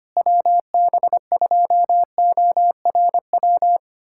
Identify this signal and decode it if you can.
Morse code, heard as a keyed tone, about 25 wpm, 700 Hz. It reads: WB2ORW
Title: USA Callsign DE Callsign 25wpm